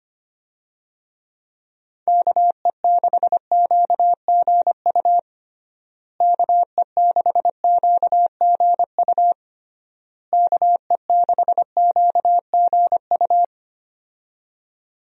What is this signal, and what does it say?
Morse code, heard as a keyed tone, about 25 wpm, 700 Hz. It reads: KE6QGU KE6QGU KE6QGU
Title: USA Call Signs - 6 Characters - Repeated 3x - 25wpm